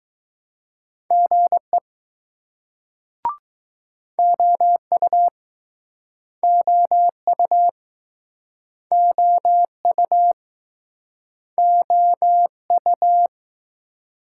Morse code, heard as a keyed tone, about 20 wpm, 700 Hz. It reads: GE OU OU OU OU